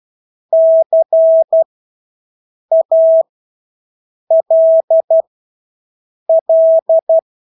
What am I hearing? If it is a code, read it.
Morse code, heard as a keyed tone, about 12 wpm, spaced as for 6 wpm, 650 Hz. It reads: CALL
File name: Engelske ord 12-6 wpm